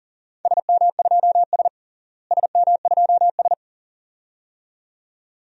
Morse code, heard as a keyed tone, about 37 wpm, 700 Hz. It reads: SM2S SM2S
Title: DXCC-100 Call Signs - 4 Characters - Repeated 2x - 40wpm